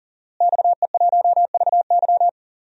Morse code, heard as a keyed tone, about 40 wpm, 700 Hz. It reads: XE1VY
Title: DXCC-100 Call Signs - 5 Characters - Repeated 2x - 40wpm